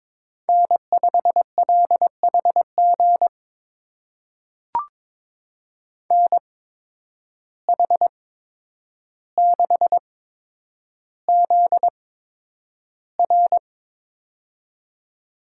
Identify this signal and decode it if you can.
Morse code, heard as a keyed tone, about 22 wpm, spaced as for 9 wpm, 700 Hz. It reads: N5LHG NH6ZR
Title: USA Call Signs - 5 Characters - 8x Character Spacing - 22wpm